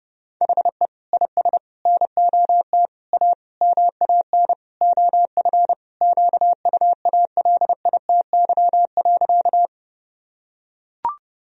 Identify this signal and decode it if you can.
Morse code, heard as a keyed tone, about 30 wpm, 700 Hz. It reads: HE IS NOT A MAN OF QUALITY.
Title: Sentences from Top 1000 Words - Review Part 4 - 30wpm